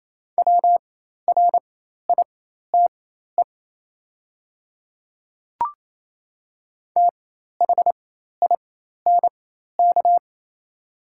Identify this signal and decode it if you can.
Morse code, heard as a keyed tone, about 28 wpm, spaced as for 13 wpm, 700 Hz. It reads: WRITE THINK
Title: 4x Character Spacing - Top 500 Words - 28wpm